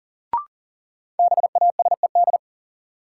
Morse code, heard as a keyed tone, about 40 wpm, 700 Hz. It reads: BASED
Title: Sets of 2 Words - New Word Drill - 40wpm